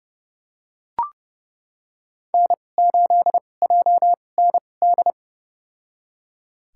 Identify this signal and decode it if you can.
Morse code, heard as a keyed tone, about 32 wpm, spaced as for 22 wpm, 700 Hz. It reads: N8JND
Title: USA Call Signs - 5 Characters - 2x Character Spacing - 30wpm